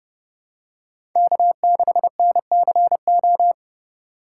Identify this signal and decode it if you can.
Morse code, heard as a keyed tone, about 30 wpm, 700 Hz. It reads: K6NCO
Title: USA Call Signs - 5 Characters - 3x Character Spacing - 30wpm